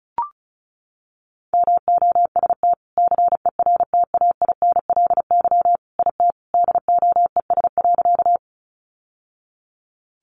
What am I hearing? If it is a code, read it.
Morse code, heard as a keyed tone, about 34 wpm, 700 Hz. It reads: MOST CERTAINLY IT DOES.
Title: Sentences from Top 900 Words 35wpm